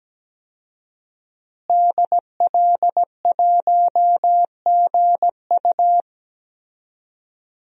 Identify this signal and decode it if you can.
Morse code, heard as a keyed tone, about 17 wpm, 700 Hz. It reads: DL1GU
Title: DXCC-100 Call Signs - 5 Characters - Repeated 2x - 17wpm